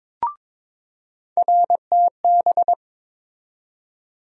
Morse code, heard as a keyed tone, about 22 wpm, 700 Hz. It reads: RTB
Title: USA Call Sign Postfixes - 3 Letters 22wpm